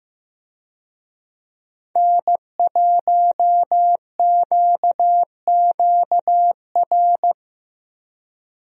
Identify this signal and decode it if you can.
Morse code, heard as a keyed tone, about 15 wpm, 700 Hz. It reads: N1QQR
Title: USA Call Signs - 5 Characters - Repeated 2x - 15wpm